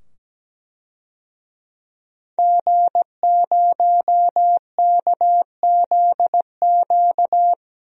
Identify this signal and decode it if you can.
Morse code, heard as a keyed tone, about 17 wpm, 700 Hz. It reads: G0KZQ
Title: DXCC-100 Call Signs - 5 Characters - 17wpm